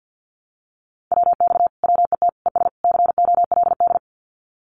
Morse code, heard as a keyed tone, about 50 wpm, 700 Hz. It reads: WX WET ES COLD
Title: Common QSO Phrases Encore 50wpm